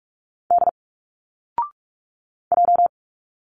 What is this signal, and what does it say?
Morse code, heard as a keyed tone, about 46 wpm, 700 Hz. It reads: D J
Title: Single Letters 45wpm